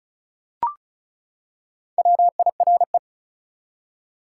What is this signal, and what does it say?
Morse code, heard as a keyed tone, about 36 wpm, 700 Hz. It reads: WIRE
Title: Top 180 QSO Elements 35wpm